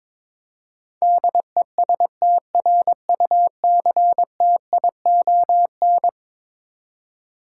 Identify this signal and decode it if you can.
Morse code, heard as a keyed tone, about 22 wpm, 700 Hz. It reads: DESTRUCTION
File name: practice / 22wpm / Eleven Letter Words - 22wpm